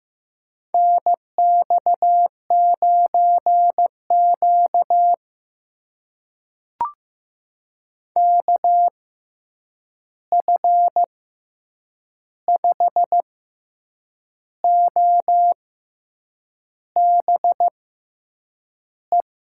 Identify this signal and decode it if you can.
Morse code, heard as a keyed tone, about 15 wpm, spaced as for 7 wpm, 700 Hz. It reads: NX9Q KF5OBE
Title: USA Call Signs - 6x Character Spacing - 15wpm